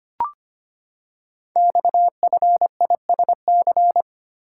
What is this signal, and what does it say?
Morse code, heard as a keyed tone, about 25 wpm, 700 Hz. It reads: XFISC
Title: Code Groups - 5 Letters - 25wpm